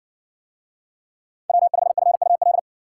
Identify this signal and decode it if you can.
Morse code, heard as a keyed tone, about 57 wpm, 700 Hz. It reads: W5FUL